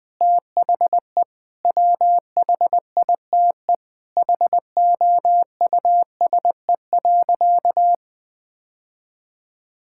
Morse code, heard as a keyed tone, about 20 wpm, 700 Hz. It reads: THE WHITE HOUSE.
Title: Sentences with 4-Word Phrases - 20wpm